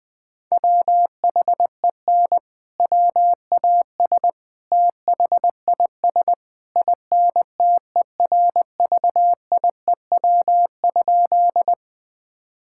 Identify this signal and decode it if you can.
Morse code, heard as a keyed tone, about 21 wpm, 700 Hz. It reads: WHEN WAS THIS INTERVIEW?
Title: Sentences from Top 1600 Words 20wpm